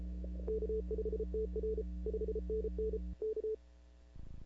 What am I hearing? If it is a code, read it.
Morse code, heard as a keyed tone, about 31 wpm, 470 Hz. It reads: K5TR 5NN K
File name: CW / 2005-k5tr arrl cw dx -15:00